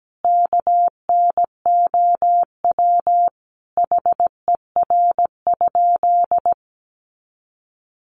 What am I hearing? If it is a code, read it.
Morse code, heard as a keyed tone, about 17 wpm, 700 Hz. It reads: KNOW HER?
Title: Sentences from Top 100 Words 17wpm